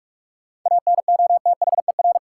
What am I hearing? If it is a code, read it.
Morse code, heard as a keyed tone, about 46 wpm, 700 Hz. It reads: ANOTHER
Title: Sets of 5 Words Encore - New Word Drill - 45wpm